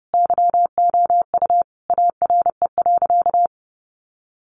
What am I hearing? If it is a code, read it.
Morse code, heard as a keyed tone, about 30 wpm, 700 Hz. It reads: YOU ARE.
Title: Sentences from Top 1100 Words 30wpm